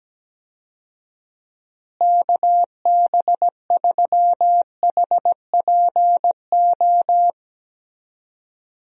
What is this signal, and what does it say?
Morse code, heard as a keyed tone, about 17 wpm, 700 Hz. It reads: KB3HPO
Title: USA Call Signs - 6 Characters - Repeated 3x - 17wpm